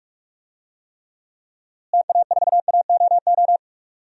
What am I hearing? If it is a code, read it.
Morse code, heard as a keyed tone, about 46 wpm, 700 Hz. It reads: TA4AOO
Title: DXCC-100 Call Signs - 6 Characters - Repeated 3x - 45wpm